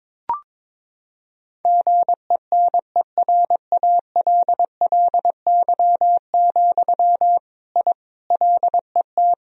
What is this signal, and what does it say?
Morse code, heard as a keyed tone, about 21 wpm, 700 Hz. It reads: GENERALLY, I LET
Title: Sentences from Top 1000 Words - Review Part 4 - 22wpm